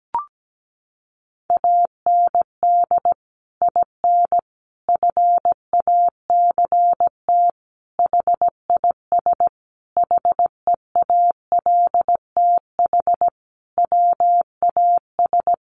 Morse code, heard as a keyed tone, about 17 wpm, 700 Hz. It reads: AND IN FACT HIS HEALTH WAS
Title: Sentences from Top 1000 Words 17wpm